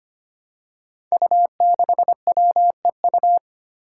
Morse code, heard as a keyed tone, about 25 wpm, 700 Hz. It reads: U6WEU
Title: Five Challenging Letter-Number Combinations 25wpm